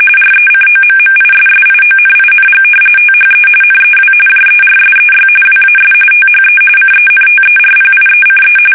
RTTY, about 45.45 baud. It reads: RECOMMISSIONING THE NEW JERSEY WAS THE CENTERPIECE
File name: fsk850 / battleshipnj